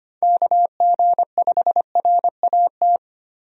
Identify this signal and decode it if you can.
Morse code, heard as a keyed tone, about 25 wpm, 700 Hz. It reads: KG5RAT
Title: USA Call Signs - 6 Characters - 25wpm